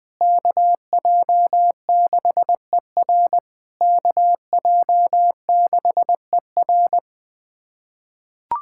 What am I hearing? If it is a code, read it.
Morse code, heard as a keyed tone, about 20 wpm, 700 Hz. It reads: KJ6ER KJ6ER
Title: POTA QSO Practice - 20wpm